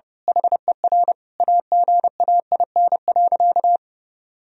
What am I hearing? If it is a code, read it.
Morse code, heard as a keyed tone, about 30 wpm, 700 Hz. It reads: HER AGAIN.
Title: Sentences from Top 1000 Words - Review Part 4 - 30wpm